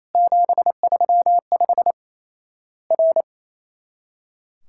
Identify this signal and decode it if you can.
Morse code, heard as a keyed tone, about 28 wpm, 700 Hz. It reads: 735 R
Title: Worked All Europe DX Contest - QTC Exchange - 28wpm